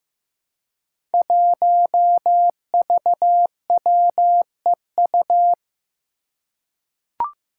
Morse code, heard as a keyed tone, about 15 wpm, 700 Hz. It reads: 1VWEU
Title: Five Challenging Letter-Number Combinations 15wpm